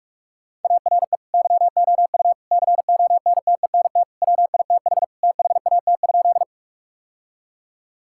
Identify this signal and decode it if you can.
Morse code, heard as a keyed tone, about 42 wpm, 700 Hz. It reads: ARE YOU CONTENT WITH THAT?
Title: Sentences from Top 1000 Words - Review Part 6 - 45wpm